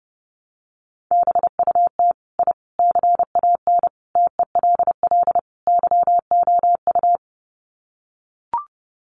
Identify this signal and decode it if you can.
Morse code, heard as a keyed tone, about 28 wpm, 700 Hz. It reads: BUT I CAN TELL YOU
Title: Sets of 5 Words Encore 30wpm